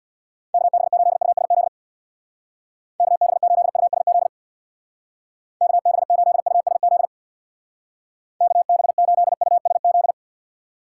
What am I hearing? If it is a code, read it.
Morse code, heard as a keyed tone, about 64 wpm, 700 Hz. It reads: KB8USZ KB8USZ KB8USZ KB8USZ